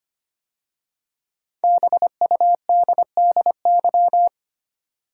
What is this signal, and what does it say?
Morse code, heard as a keyed tone, about 25 wpm, 700 Hz. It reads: BUDDY
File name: 5 Letter Words - First 3 Letters Given - 25wpm